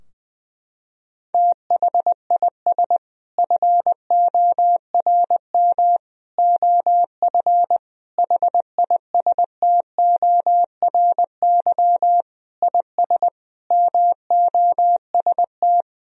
Morse code, heard as a keyed tone, about 20 wpm, 700 Hz. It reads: THIS FORM OF HISTORY IS MOST
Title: Sentences from Top 1000 Words - Review Part 4 - 20wpm